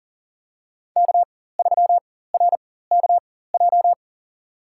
Morse code, heard as a keyed tone, about 40 wpm, spaced as for 19 wpm, 700 Hz. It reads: K3RKJ
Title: USA Call Signs - 5 Characters - 4x Character Spacing - 40wpm